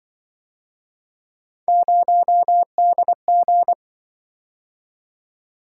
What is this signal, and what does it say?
Morse code, heard as a keyed tone, about 24 wpm, 700 Hz. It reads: 0DG
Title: Speed-Racing - Three Letter-Number Combinations - 20wpm